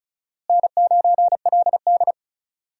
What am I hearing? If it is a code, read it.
Morse code, heard as a keyed tone, about 36 wpm, 700 Hz. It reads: N9LD
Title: USA Call Signs - 4 Characters - Repeated 3x - 35wpm